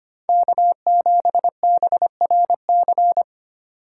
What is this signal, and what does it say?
Morse code, heard as a keyed tone, about 25 wpm, 700 Hz. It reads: K7BRC
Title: USA Call Signs - 5 Characters - 4x Character Spacing - 25wpm